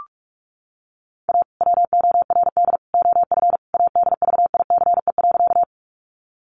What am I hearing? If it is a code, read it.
Morse code, heard as a keyed tone, about 44 wpm, 700 Hz. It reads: A WORD OF ADVICE.
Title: Sentences from Top 1100 Words 45wpm